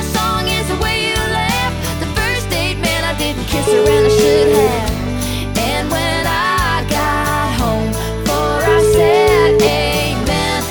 Ringback tone (Europe, 425 Hz), with noise.